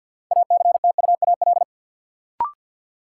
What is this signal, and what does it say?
Morse code, heard as a keyed tone, about 50 wpm, 700 Hz. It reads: ACTUAL
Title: Sentences from Top 1000 Words - Review Part 5 - New Word Drill - 50wpm